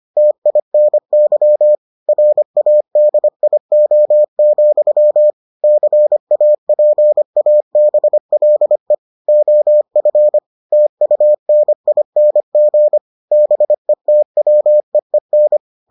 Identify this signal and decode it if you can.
Morse code, heard as a keyed tone, about 25 wpm, 600 Hz. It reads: TINY RADIO, CAPABLE OF TUNING BETWEEN